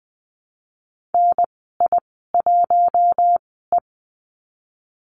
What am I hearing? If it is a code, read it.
Morse code, heard as a keyed tone, about 20 wpm, spaced as for 14 wpm, 700 Hz. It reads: NI1E
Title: 082 Lesson 14g - Number 2 - Call Signs 20wpm